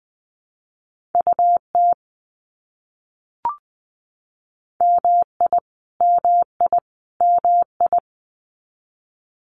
Morse code, heard as a keyed tone, about 20 wpm, 700 Hz. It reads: UT MI MI MI
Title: US States Repeated 3x 20wpm